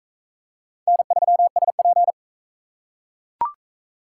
Morse code, heard as a keyed tone, about 41 wpm, 700 Hz. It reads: N3SP